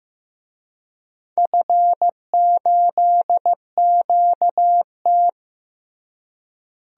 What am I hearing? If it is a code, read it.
Morse code, heard as a keyed tone, about 15 wpm, 700 Hz. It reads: F8QT
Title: DXCC-100 Call Signs - 4 Characters - Repeated 3x - 15wpm